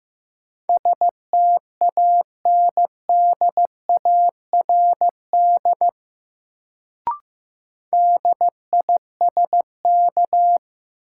Morse code, heard as a keyed tone, about 15 wpm, 700 Hz. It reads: STANDARD DISK
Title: Sentences from Top 1000 Words - Review Part 6 - New Word Drill - 15wpm